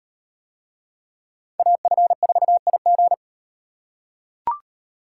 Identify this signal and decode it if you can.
Morse code, heard as a keyed tone, about 35 wpm, 700 Hz. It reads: AF4IG